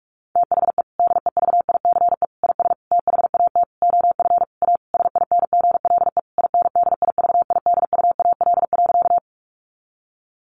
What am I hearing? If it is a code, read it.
Morse code, heard as a keyed tone, about 44 wpm, 700 Hz. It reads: THE DEVICE IS THAT OF A SINGLE INDIVIDUAL.
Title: Sentences from Top 1500 Words 45wpm